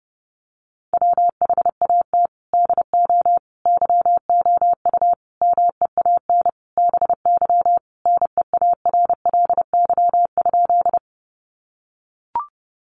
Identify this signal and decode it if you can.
Morse code, heard as a keyed tone, about 30 wpm, 700 Hz. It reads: WHAT DO YOU MEAN BY NEARLY?
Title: Sentences from Top 1300 Words 30wpm